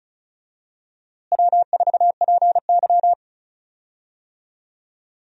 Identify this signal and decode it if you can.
Morse code, heard as a keyed tone, about 36 wpm, 700 Hz. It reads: W4PY
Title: USA Call Signs - 4 Characters - Repeated 3x - 35wpm